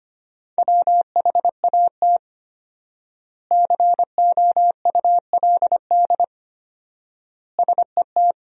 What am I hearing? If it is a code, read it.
Morse code, heard as a keyed tone, about 25 wpm, 700 Hz. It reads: WHAT COULD SET
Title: 4x Word-Spacing - Sentences from Top 200 Words - 25wpm